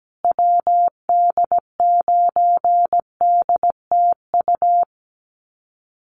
Morse code, heard as a keyed tone, about 17 wpm, 700 Hz. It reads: WD9DTU
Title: USA Call Signs Repeated 3x 17wpm